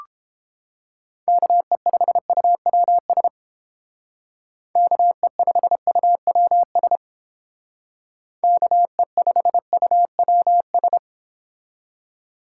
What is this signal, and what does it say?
Morse code, heard as a keyed tone, about 29 wpm, 700 Hz. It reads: KE5UWS KE5UWS KE5UWS